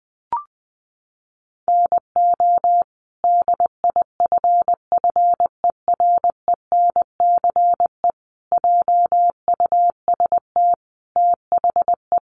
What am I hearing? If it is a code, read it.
Morse code, heard as a keyed tone, about 20 wpm, 700 Hz. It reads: NO DIFFERENCE JUST THE
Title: Sentences from Top 800 Words 20wpm